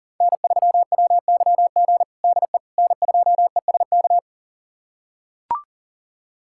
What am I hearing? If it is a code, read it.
Morse code, heard as a keyed tone, about 40 wpm, 700 Hz. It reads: N3WYG DE N2ESK
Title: USA Callsign DE Callsign - 5 Characters - 40wpm